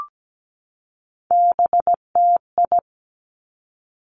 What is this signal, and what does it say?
Morse code, heard as a keyed tone, about 17 wpm, 700 Hz. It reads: BTI
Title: Three Challenging Letter-Number Combinations 17wpm